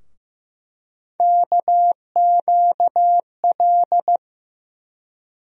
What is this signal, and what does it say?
Morse code, heard as a keyed tone, about 15 wpm, 700 Hz. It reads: KQL
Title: USA Call Sign Postfixes - 3 Letters 15wpm